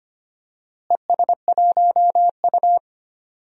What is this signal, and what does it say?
Morse code, heard as a keyed tone, about 25 wpm, 700 Hz. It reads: ES1U
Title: DXCC-100 Call Signs - 4 Characters - Repeated 2x - 25wpm